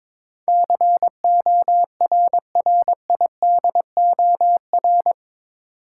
Morse code, heard as a keyed tone, about 22 wpm, 700 Hz. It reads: CORRIDOR
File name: practice / 22wpm / Eight Letter Words - 22wpm